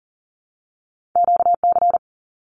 Morse code, heard as a keyed tone, about 39 wpm, 700 Hz. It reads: QC
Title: Canadian Provinces and Territories 40wpm